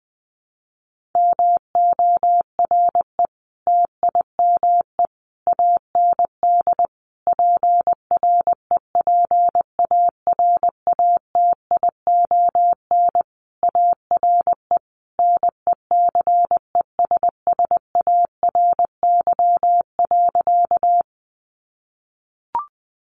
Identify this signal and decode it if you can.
Morse code, heard as a keyed tone, about 20 wpm, 700 Hz. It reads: MORE TIME AND PREPARATION ARE NECESSARY.
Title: Sentences from Top 1400 Words 20wpm